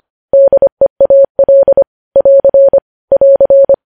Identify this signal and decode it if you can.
Morse code, heard as a keyed tone, about 25 wpm, 550 Hz. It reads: DEAL <AR> <AR>
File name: podcast / News headlines-1x-25x25